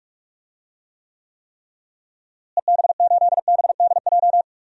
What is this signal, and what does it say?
Morse code, heard as a keyed tone, about 42 wpm, 700 Hz. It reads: EB8BDJ
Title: DXCC-100 Call Signs - 6 Characters - Repeated 3x - 45wpm